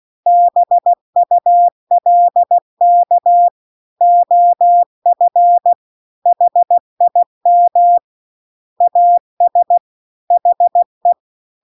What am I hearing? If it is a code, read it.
Morse code, heard as a keyed tone, about 16 wpm, 700 Hz. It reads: BULK OF HIM AS HE